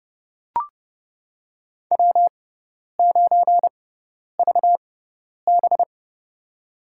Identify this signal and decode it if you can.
Morse code, heard as a keyed tone, about 29 wpm, spaced as for 10 wpm, 700 Hz. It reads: W9VB
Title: USA Call Signs - 4 Characters - 6x Character Spacing - 30wpm